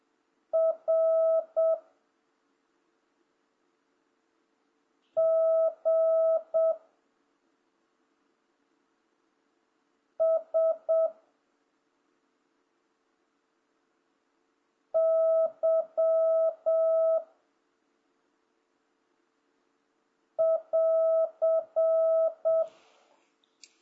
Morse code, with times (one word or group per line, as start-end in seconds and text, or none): 0.5-1.8 R
5.2-6.7 G
10.2-11.1 S
14.9-17.2 Y
20.4-22.6 <AR>